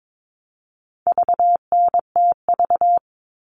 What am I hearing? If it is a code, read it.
Morse code, heard as a keyed tone, about 22 wpm, 700 Hz. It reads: VNTV
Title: All Cut Numbers - 4-Digit 22wpm